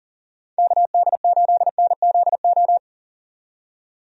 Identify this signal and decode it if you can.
Morse code, heard as a keyed tone, about 40 wpm, 700 Hz. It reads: KD8NZO
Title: USA Call Signs - 6 Characters - 40wpm